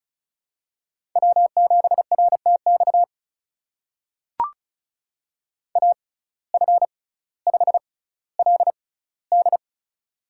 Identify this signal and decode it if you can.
Morse code, heard as a keyed tone, about 36 wpm, spaced as for 18 wpm, 700 Hz. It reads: W7RTX AF5LD
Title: USA Call Signs - 5 Characters - 6x Character Spacing - 35wpm